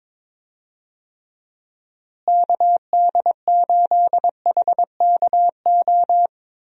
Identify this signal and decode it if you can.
Morse code, heard as a keyed tone, about 22 wpm, 700 Hz. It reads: KD8HKO
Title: USA Call Signs - 6 Characters - Repeated 3x - 22wpm